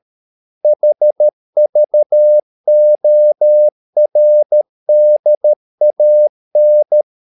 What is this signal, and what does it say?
Morse code, heard as a keyed tone, about 13 wpm, 600 Hz. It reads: HVORDAN
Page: Ferie 13 wpm